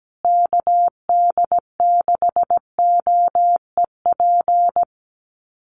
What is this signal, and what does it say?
Morse code, heard as a keyed tone, about 17 wpm, 700 Hz. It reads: KD6OEP
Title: USA Call Signs Repeated 2x 17wpm